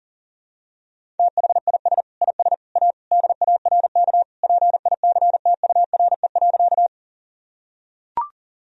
Morse code, heard as a keyed tone, about 40 wpm, 700 Hz. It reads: THIS IS A DARK PICTURE.
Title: Sentences from Top 1000 Words - Review Part 1 - 40wpm